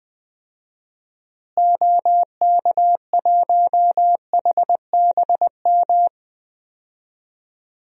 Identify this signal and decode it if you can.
Morse code, heard as a keyed tone, about 20 wpm, 700 Hz. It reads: OK1HBM
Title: DXCC-100 Call Signs - 6 Characters - Repeated 2x - 20wpm